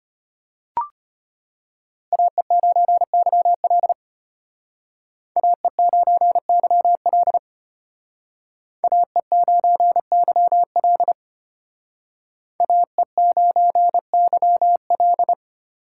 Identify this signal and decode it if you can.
Morse code, heard as a keyed tone, about 32 wpm, 700 Hz. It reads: AE9YL AE9YL AE9YL AE9YL